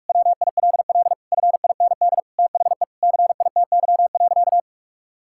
Morse code, heard as a keyed tone, about 44 wpm, 700 Hz. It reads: WILL FIND THE CITY.
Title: Sentences from Top 1000 Words - Review Part 6 - 45wpm